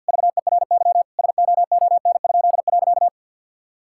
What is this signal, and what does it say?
Morse code, heard as a keyed tone, about 50 wpm, 700 Hz. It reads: VERY SOON?.
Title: Pangram Challenge v2 - 50wpm